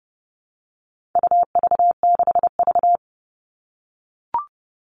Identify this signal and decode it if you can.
Morse code, heard as a keyed tone, about 30 wpm, 700 Hz. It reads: UV6V
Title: All Cut Numbers - 4-Digit 30wpm